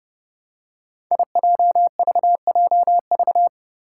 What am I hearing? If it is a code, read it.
Morse code, heard as a keyed tone, about 30 wpm, 700 Hz. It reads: IJVJV
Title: Five Challenging Letter-Number Combinations 30wpm